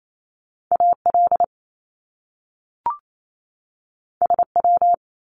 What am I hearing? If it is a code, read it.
Morse code, heard as a keyed tone, about 28 wpm, 700 Hz. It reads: AL SW
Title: Top Two Letter Combinations in Words 28wpm